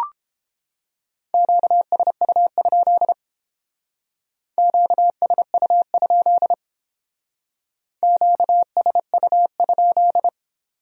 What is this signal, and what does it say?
Morse code, heard as a keyed tone, about 30 wpm, 700 Hz. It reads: QSU? QSU? QSU?